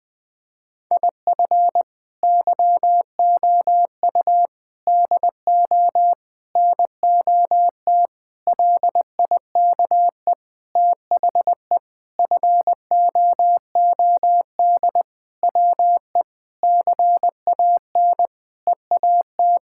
Morse code, heard as a keyed tone, about 20 wpm, 700 Hz. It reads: IF YOU DO NOT LIKE THE FOOD WE CAN EAT
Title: Sentences with 5-Word Phrases - Spoken Beginning - 20wpm